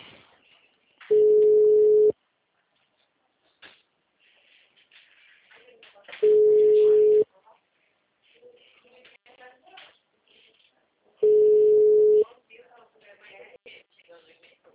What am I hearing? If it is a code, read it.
Ringback tone (Europe, 425 Hz), with speech.
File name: out- -2013- - - .37818